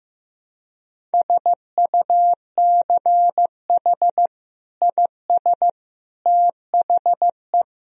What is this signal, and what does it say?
Morse code, heard as a keyed tone, about 15 wpm, 700 Hz. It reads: SUCH IS THE
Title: Sentences from Top 1000 Words - Review Part 3 - 15wpm